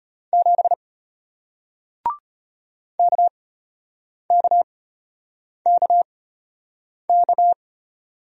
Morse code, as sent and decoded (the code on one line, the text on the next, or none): --... / -.- -.- -.- -.-
7 KKKK